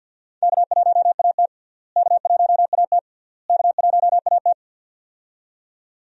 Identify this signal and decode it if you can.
Morse code, heard as a keyed tone, about 47 wpm, 700 Hz. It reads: K1AT K1AT K1AT
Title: USA Call Signs - 4 Characters - Repeated 3x - 50wpm